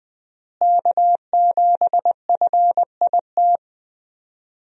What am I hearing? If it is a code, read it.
Morse code, heard as a keyed tone, about 20 wpm, 700 Hz. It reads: K7FIT
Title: USA Call Signs - 5 Characters - Repeated 3x - 20wpm